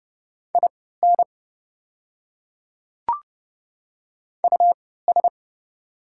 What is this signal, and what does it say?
Morse code, heard as a keyed tone, about 30 wpm, spaced as for 17 wpm, 700 Hz. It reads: IN US
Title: 3x Character Spacing - Top 300 Words - 30wpm